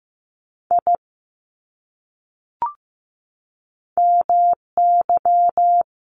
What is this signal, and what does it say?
Morse code, heard as a keyed tone, about 15 wpm, 700 Hz. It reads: I MY
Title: Top 500 Words 15wpm